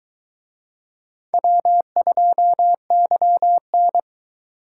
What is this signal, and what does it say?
Morse code, heard as a keyed tone, about 22 wpm, 700 Hz. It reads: W2YN